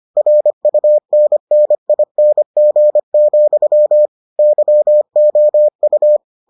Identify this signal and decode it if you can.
Morse code, heard as a keyed tone, about 25 wpm, 600 Hz. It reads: RUNNING, YOU